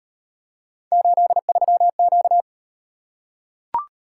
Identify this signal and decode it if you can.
Morse code, heard as a keyed tone, about 38 wpm, 700 Hz. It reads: 83Q